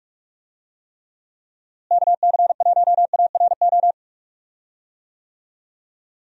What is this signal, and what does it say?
Morse code, heard as a keyed tone, about 43 wpm, 700 Hz. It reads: KC1ARO
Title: USA Call Signs - 6 Characters - Repeated 3x - 45wpm